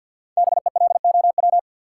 Morse code, heard as a keyed tone, about 53 wpm, 700 Hz. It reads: BELOW
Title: Sentences from Top 1200 Words - New Word Drill Alt - 50wpm